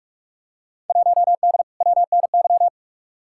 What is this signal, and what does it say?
Morse code, heard as a keyed tone, about 46 wpm, 700 Hz. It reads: 1D WNY
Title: Field Day Exchanges - 45wpm